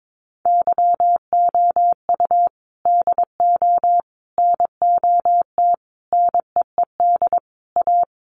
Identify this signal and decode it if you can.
Morse code, heard as a keyed tone, about 22 wpm, 700 Hz. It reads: YOU DO NOT NEED A
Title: Sets of 5 Words Encore 22wpm